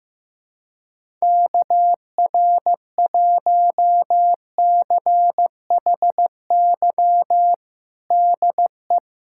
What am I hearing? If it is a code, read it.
Morse code, heard as a keyed tone, about 15 wpm, 700 Hz. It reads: KR1CHY DE
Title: USA Callsign DE Callsign - 6 Characters - 15wpm